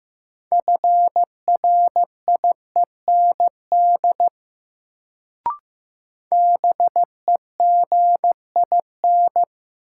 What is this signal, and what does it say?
Morse code, heard as a keyed tone, about 15 wpm, 700 Hz. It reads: FRIEND BEGIN
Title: Sentences from Top 300 Words - New Word Drill - 15wpm